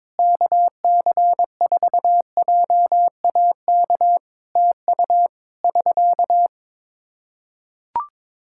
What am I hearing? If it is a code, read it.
Morse code, heard as a keyed tone, about 22 wpm, 700 Hz. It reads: KC4JAK TU <SK>
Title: Practice QSO - Exchange 4 - Ending - 22wpm